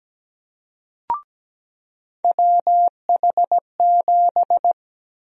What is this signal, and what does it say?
Morse code, heard as a keyed tone, about 17 wpm, 700 Hz. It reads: WH7
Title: USA Call Sign Prefixes - 17wpm